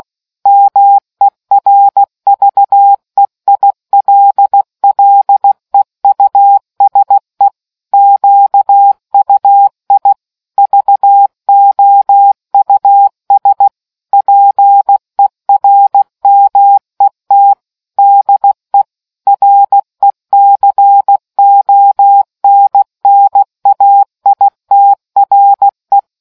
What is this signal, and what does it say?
Morse code, heard as a keyed tone, about 16 wpm, 800 Hz. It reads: MERVEILLEUSE QUI VOUS PERMET DE RECONNAITRE